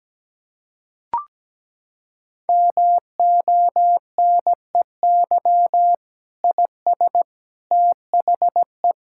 Morse code, heard as a keyed tone, about 17 wpm, 700 Hz. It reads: MONEY IS THE
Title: Sentences from Top 1000 Words - Review Part 5 - 17wpm